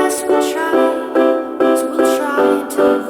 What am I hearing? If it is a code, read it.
There is no code.